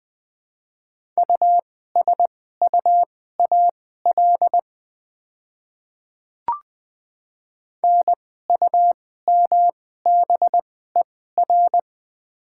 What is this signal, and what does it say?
Morse code, heard as a keyed tone, about 20 wpm, spaced as for 14 wpm, 700 Hz. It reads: USUAL NUMBER
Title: 2x Character Spacing - Top 500 Words - 20wpm